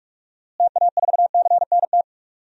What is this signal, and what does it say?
Morse code, heard as a keyed tone, about 42 wpm, 690 Hz. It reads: TA4CNT
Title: DXCC-100 Call Signs - 6 Characters - Repeated 2x - 45wpm